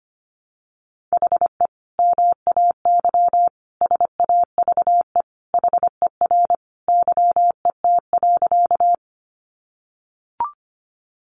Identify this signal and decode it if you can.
Morse code, heard as a keyed tone, about 25 wpm, 700 Hz. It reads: HE MAY SAVE HER YET.
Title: Sentences from Top 1100 Words 25wpm